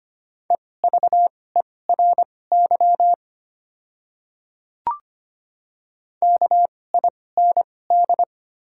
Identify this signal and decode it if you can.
Morse code, heard as a keyed tone, about 25 wpm, spaced as for 18 wpm, 700 Hz. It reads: EVERY KIND
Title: 2x Character Spacing - Top 500 Words - 25wpm